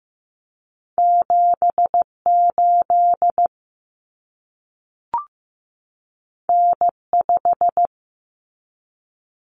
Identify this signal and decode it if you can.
Morse code, heard as a keyed tone, about 15 wpm, 700 Hz. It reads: 78 N5
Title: Common Cut Numbers - 2-Digit 15wpm